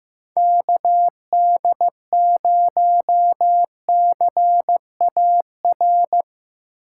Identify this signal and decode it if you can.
Morse code, heard as a keyed tone, about 15 wpm, 700 Hz. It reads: KD0CAR
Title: USA Call Signs - 6x Character Spacing - 15wpm